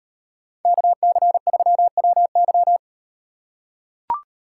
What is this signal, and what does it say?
Morse code, heard as a keyed tone, about 38 wpm, 700 Hz. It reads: KC3WY